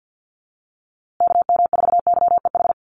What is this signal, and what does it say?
Morse code, heard as a keyed tone, about 50 wpm, 700 Hz. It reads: KM4YEH
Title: USA Call Signs Repeated 2x 50wpm